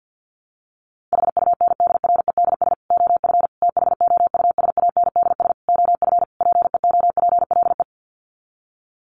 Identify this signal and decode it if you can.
Morse code, heard as a keyed tone, about 50 wpm, 700 Hz. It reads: HUNDREDS OF THOUSANDS OF PEOPLE
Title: Sets of 5 Words Encore2 50wpm